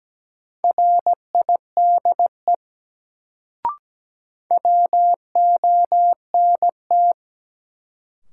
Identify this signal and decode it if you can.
Morse code, heard as a keyed tone, about 17 wpm, 700 Hz. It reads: RIDE WONT
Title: Sentences from Top 1000 Words - Review Part 6 - New Word Drill - 17wpm